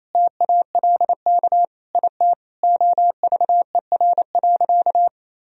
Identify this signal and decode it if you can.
Morse code, heard as a keyed tone, about 28 wpm, 700 Hz. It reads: TALK IT OVER.
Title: Sentences from Top 500 Words and Abbreviations 28wpm